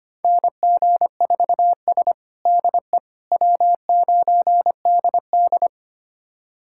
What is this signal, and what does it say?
Morse code, heard as a keyed tone, about 25 wpm, 700 Hz. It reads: NG4S DE W9DD
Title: USA Callsign DE Callsign - 4 Characters - 25wpm